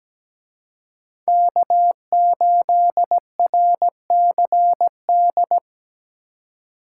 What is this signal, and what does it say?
Morse code, heard as a keyed tone, about 20 wpm, 700 Hz. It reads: K8RCD
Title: Speed Racing - USA 1x3 Call Signs 17wpm